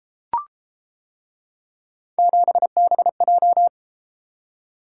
Morse code, heard as a keyed tone, about 32 wpm, 700 Hz. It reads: 7BJ